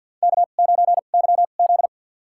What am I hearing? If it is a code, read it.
Morse code, heard as a keyed tone, about 52 wpm, spaced as for 36 wpm, 700 Hz. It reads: K9YZ
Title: USA Call Signs - 4 Characters - 2x Character Spacing - 50wpm